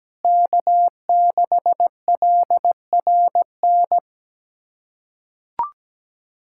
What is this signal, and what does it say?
Morse code, heard as a keyed tone, about 17 wpm, 700 Hz. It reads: K6LRN
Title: USA Callsign DE Callsign 17wpm